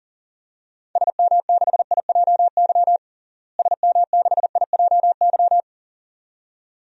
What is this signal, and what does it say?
Morse code, heard as a keyed tone, about 40 wpm, 700 Hz. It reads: SM6IJY SM6IJY
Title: DXCC-100 Call Signs - 6 Characters - Repeated 2x - 40wpm